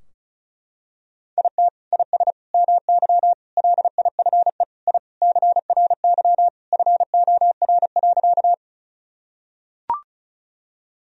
Morse code, heard as a keyed tone, about 34 wpm, 700 Hz. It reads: IT IS MY LIFE I CRY FOR.
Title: Sentences from Top 1000 Words - Review Part 2 - 35wpm